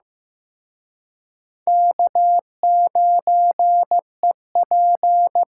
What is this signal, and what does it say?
Morse code, heard as a keyed tone, about 18 wpm, 700 Hz. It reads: K9EP
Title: Speed Racing - USA 1x2 Call Signs 15wpm